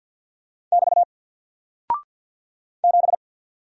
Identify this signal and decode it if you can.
Morse code, heard as a keyed tone, about 48 wpm, 700 Hz. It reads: <BT> 7
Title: Single Character (Letter:Number:Punctuation:Prosign) Alt 50wpm